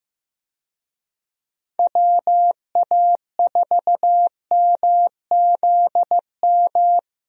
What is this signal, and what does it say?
Morse code, heard as a keyed tone, about 15 wpm, 700 Hz. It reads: WA4MZM
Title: USA Call Signs - 6 Characters - 15wpm